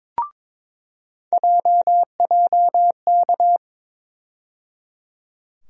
Morse code, heard as a keyed tone, about 22 wpm, 700 Hz. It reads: JJK
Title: USA Call Sign Postfixes - 3 Letters 22wpm